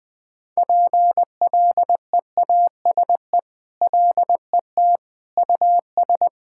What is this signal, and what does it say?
Morse code, heard as a keyed tone, about 20 wpm, 700 Hz. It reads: PLEASE LET US
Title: Sentences with 6-Word Phrases - 20wpm